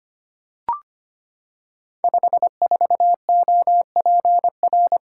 Morse code, heard as a keyed tone, about 25 wpm, 700 Hz. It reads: 54OPR
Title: Code Groups - 5 Letter:Numbers - 25wpm